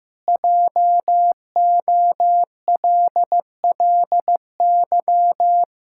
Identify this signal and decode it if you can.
Morse code, heard as a keyed tone, about 15 wpm, 700 Hz. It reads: JOLLY
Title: Pangram Challenge v2 - 15wpm